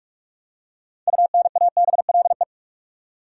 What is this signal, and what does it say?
Morse code, heard as a keyed tone, about 46 wpm, 700 Hz. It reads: UNABLE
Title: Six Letter Words - 45wpm